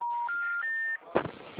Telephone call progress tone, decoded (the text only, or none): special information tone